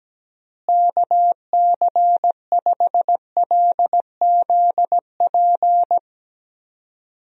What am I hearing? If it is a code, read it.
Morse code, heard as a keyed tone, about 17 wpm, 700 Hz. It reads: KC5LZP
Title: USA Call Signs - 6 Characters - Repeated 2x - 17wpm